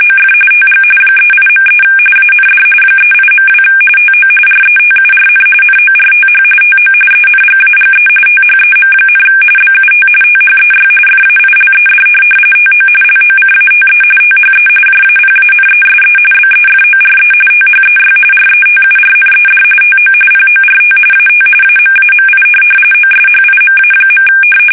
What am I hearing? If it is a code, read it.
RTTY, about 45.45 baud. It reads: OF TG 38.3 AND 38.2 TO RETURN NORTH TO ENGAGE THEM, BUT THEY FAILED TO LOCATE THE JAPANESE BEFORE THEY RETURNED TO PORT AT AMAMI OSHIMA.
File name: fsk850 / USS-Massachusetts